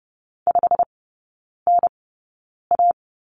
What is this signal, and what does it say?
Morse code, heard as a keyed tone, about 30 wpm, spaced as for 9 wpm, 700 Hz. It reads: 5NA
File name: Lesson 18e - Letter m - Three Characters 30wpm